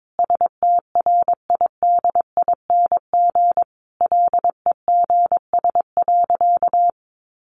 Morse code, heard as a keyed tone, about 22 wpm, 700 Hz. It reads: STRIDING LEGS.
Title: A Sound Of Thunder 22wpm